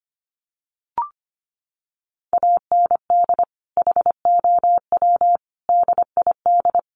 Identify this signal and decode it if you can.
Morse code, heard as a keyed tone, about 25 wpm, 700 Hz. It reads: AND HOW DID
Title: Sentences from Top 1200 Words 25wpm